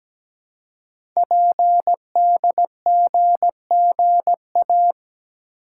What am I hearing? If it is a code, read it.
Morse code, heard as a keyed tone, about 17 wpm, 700 Hz. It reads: PDGGA
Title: Code Groups - 5 Letters - 17wpm